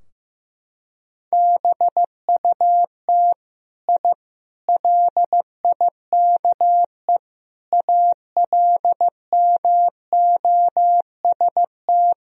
Morse code, heard as a keyed tone, about 15 wpm, 700 Hz. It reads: BUT I LIKE ALMOST
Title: Sentences from Top 1000 Words - Review Part 4 - 15wpm